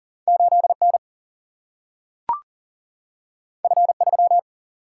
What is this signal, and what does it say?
Morse code, heard as a keyed tone, about 40 wpm, 700 Hz. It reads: 8 N F 3
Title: Single-Letter Single-Number 40wpm